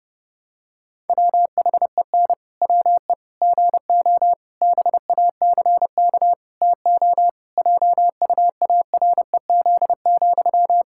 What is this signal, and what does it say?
Morse code, heard as a keyed tone, about 30 wpm, 700 Hz. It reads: WHEN WE GO BACK TO JUAREZ,
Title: Pangram Challenge v2 - 30wpm